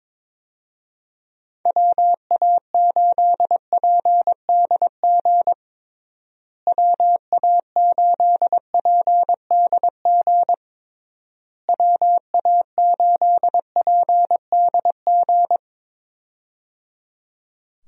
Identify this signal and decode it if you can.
Morse code, heard as a keyed tone, about 23 wpm, 700 Hz. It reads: WA8PDG WA8PDG WA8PDG
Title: USA Call Signs - 6 Characters - Repeated 3x - 22wpm